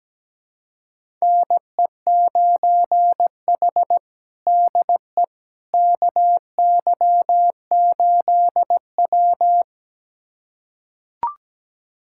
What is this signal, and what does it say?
Morse code, heard as a keyed tone, about 17 wpm, 700 Hz. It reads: NE9H DE KY8W
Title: USA Callsign DE Callsign - 4 Characters - 17wpm